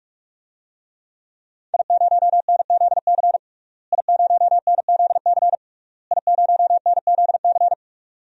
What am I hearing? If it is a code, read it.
Morse code, heard as a keyed tone, about 42 wpm, 700 Hz. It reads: I0NZC I0NZC I0NZC
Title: DXCC-100 Call Signs - 5 Characters - Repeated 3x - 45wpm